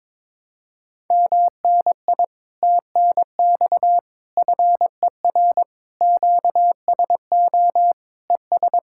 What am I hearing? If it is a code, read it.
Morse code, heard as a keyed tone, about 22 wpm, 700 Hz. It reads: MNI TNX FER QSO ES
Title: Speed-Racing Common QSO Phrases Encore - 22wpm